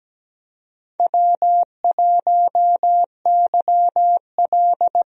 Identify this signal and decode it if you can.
Morse code, heard as a keyed tone, about 17 wpm, 700 Hz. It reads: W1YL
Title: USA Call Signs - 4 Characters - 17wpm